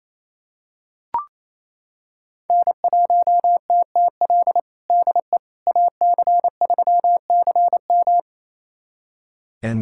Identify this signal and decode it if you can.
Morse code, heard as a keyed tone, about 28 wpm, 700 Hz. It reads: N1TTL DE AC3CM
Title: USA Callsign DE Callsign - 5 Characters - 28wpm